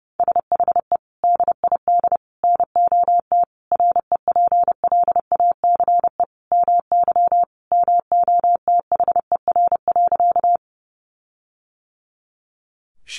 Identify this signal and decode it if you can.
Morse code, heard as a keyed tone, about 28 wpm, 700 Hz. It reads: SHE DID NOT REPLACE MY MOTHER.
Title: Sentences from Top 1400 Words 30wpm